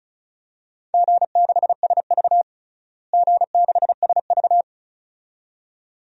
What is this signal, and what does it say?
Morse code, heard as a keyed tone, about 34 wpm, 700 Hz. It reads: G6SV G6SV
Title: DXCC-100 Call Signs - 4 Characters - Repeated 2x - 35wpm